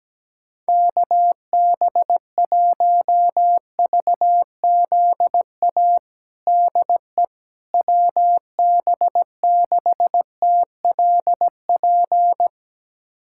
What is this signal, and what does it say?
Morse code, heard as a keyed tone, about 17 wpm, 700 Hz. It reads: KB1VZA DE WB6TLP
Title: USA Callsign DE Callsign - 6 Characters - 17wpm